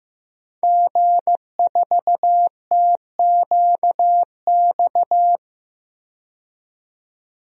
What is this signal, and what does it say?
Morse code, heard as a keyed tone, about 15 wpm, 700 Hz. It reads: G4TQX
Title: DXCC-100 Call Signs - 5 Characters - 15wpm